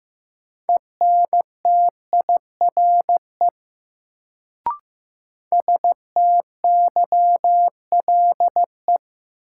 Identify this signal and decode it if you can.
Morse code, heard as a keyed tone, about 15 wpm, 700 Hz. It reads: ENTIRE STYLE
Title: Sentences from Top 800 Words - New Word Drill - 15wpm